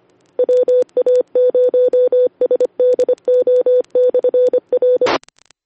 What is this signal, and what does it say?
Morse code, heard as a keyed tone, about 25 wpm, 500 Hz. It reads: WA0SDO/R
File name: sedaliarepeater